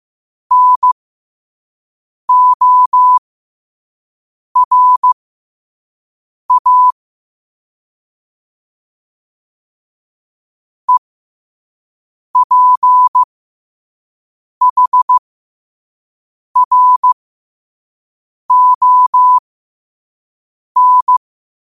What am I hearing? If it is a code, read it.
Morse code, heard as a keyed tone, about 15 wpm, spaced as for 5 wpm, 1000 Hz. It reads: NORA EPHRON
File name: qotdfiles / 5wpm